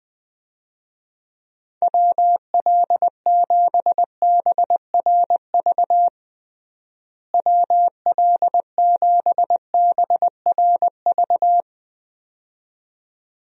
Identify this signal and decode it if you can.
Morse code, heard as a keyed tone, about 20 wpm, 700 Hz. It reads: WL7BRV WL7BRV
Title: USA Call Signs - 6 Characters - Repeated 2x - 20wpm